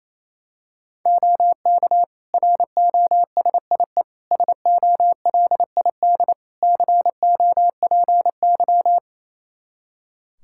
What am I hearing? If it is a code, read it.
Morse code, heard as a keyed tone, about 27 wpm, 700 Hz. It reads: OK ROSIE SOLID COPY
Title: Practice QSO - Exchange 3 - More Info - 28wpm